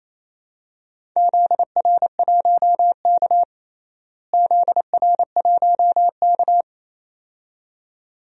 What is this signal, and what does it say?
Morse code, heard as a keyed tone, about 28 wpm, 700 Hz. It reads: ZR1K ZR1K
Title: DXCC-100 Call Signs - 4 Characters - Repeated 2x - 28wpm